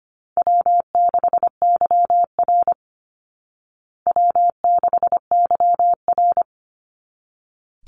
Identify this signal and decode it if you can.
Morse code, heard as a keyed tone, about 25 wpm, 700 Hz. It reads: W6YR W6YR
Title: Call Signs Repeated 3x - 25wpm